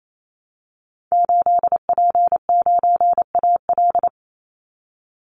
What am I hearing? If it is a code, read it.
Morse code, heard as a keyed tone, about 28 wpm, 700 Hz. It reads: 8P9AL
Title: Call Signs Repeated 3x - 28wpm